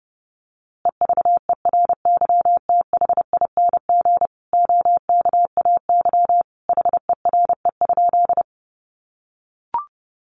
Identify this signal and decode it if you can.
Morse code, heard as a keyed tone, about 30 wpm, 700 Hz. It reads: EVERYTHING OKAY HERE?
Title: A Sound Of Thunder 30wpm